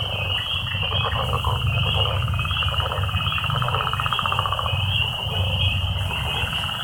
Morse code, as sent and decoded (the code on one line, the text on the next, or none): none